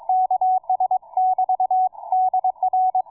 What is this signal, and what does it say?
Morse code, heard as a keyed tone, about 22 wpm, 740 Hz, with radio noise. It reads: KS <BT> DR